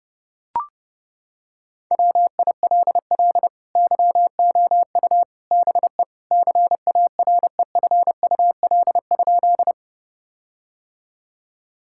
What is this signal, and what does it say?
Morse code, heard as a keyed tone, about 30 wpm, 700 Hz. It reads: WILL YOU BE CAREFUL?
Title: Sentences from Top 1600 Words 30wpm